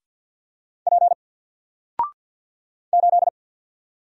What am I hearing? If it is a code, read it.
Morse code, heard as a keyed tone, about 48 wpm, 700 Hz. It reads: P 8
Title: Single Letter-Number 50wpm